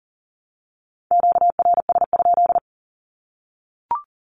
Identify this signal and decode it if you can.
Morse code, heard as a keyed tone, about 40 wpm, 700 Hz. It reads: QRS?
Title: Q-Codes Repeated 3x 40wpm